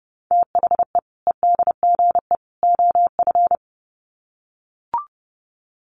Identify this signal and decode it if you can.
Morse code, heard as a keyed tone, about 30 wpm, 700 Hz. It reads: THE EDGE OF
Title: Sets of 3 Words 30wpm